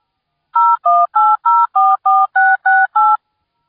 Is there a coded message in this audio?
Yes: DTMF from 0.5 to 3.2 s, heard as telephone touch tones, tones of about 220 ms. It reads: 018044668